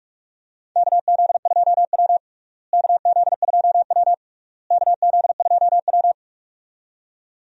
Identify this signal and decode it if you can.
Morse code, heard as a keyed tone, about 46 wpm, 700 Hz. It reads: KZ2W KZ2W KZ2W
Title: USA Call Signs - 4 Characters - Repeated 3x - 45wpm